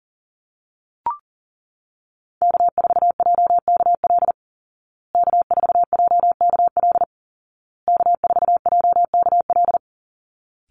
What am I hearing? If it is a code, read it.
Morse code, heard as a keyed tone, about 40 wpm, 700 Hz. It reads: K4JKL K4JKL K4JKL
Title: Call Signs Repeated 2x - 40wpm